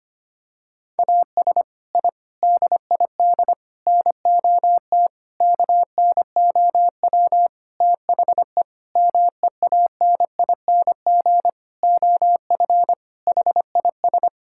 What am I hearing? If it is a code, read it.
Morse code, heard as a keyed tone, about 25 wpm, 700 Hz. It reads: AS I DID NOT KNOW THE MEANING OF HIS
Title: Sentences from Top 1000 Words - Review Part 6 - 25wpm